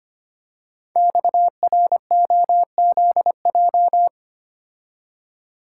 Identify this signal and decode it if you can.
Morse code, heard as a keyed tone, about 25 wpm, 700 Hz. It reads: XROZJ
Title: Code Groups - 5 Letters - 25wpm